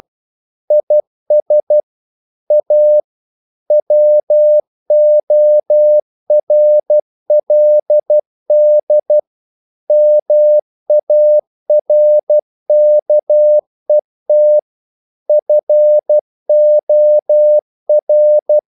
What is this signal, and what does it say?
Morse code, heard as a keyed tone, about 12 wpm, 600 Hz. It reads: IS A WORLD MARKET FOR
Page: English quotations in CW (telegraphy - morse)